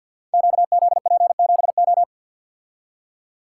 Morse code, heard as a keyed tone, about 50 wpm, 700 Hz. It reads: QZP7O
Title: Code Groups - 5 Letter:Numbers - 50wpm